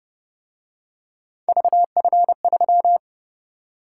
Morse code, heard as a keyed tone, about 29 wpm, 700 Hz. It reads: VF3